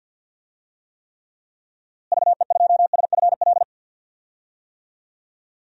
Morse code, heard as a keyed tone, about 51 wpm, 700 Hz. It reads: VE2SFL
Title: DXCC-100 Call Signs - 6 Characters - Repeated 3x - 50wpm